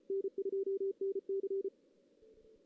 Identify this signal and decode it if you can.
Morse code, heard as a keyed tone, about 33 wpm, 390 Hz. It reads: N2NC
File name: 2016 JAN CW NAQP / 2016 Jan naqp k5ot - 0-2